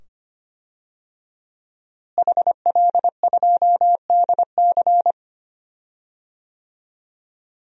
Morse code, heard as a keyed tone, about 25 wpm, 700 Hz. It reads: HL2DC
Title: DXCC-100 Call Signs - 5 Characters - Repeated 2x - 25wpm